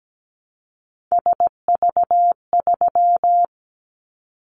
Morse code, heard as a keyed tone, about 17 wpm, 700 Hz. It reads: SV3
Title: Callsign Prefixes for Top 50 DXCC - 17wpm